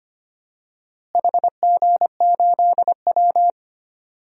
Morse code, heard as a keyed tone, about 25 wpm, 700 Hz. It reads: HG8W
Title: DXCC-100 Call Signs - 4 Characters - Repeated 3x - 25wpm